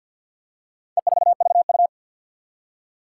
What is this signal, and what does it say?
Morse code, heard as a keyed tone, about 51 wpm, 700 Hz. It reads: E4VU
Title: Four Challenging Letter-Number Combinations 50wpm